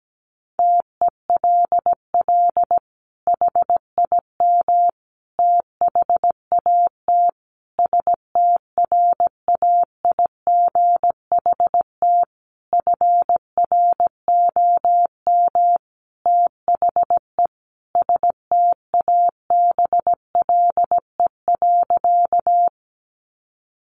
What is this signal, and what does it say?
Morse code, heard as a keyed tone, about 17 wpm, 700 Hz. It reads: TELL HIM THAT STRAIGHT FROM THE STABLE.
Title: Sentences from Top 1500 Words 17wpm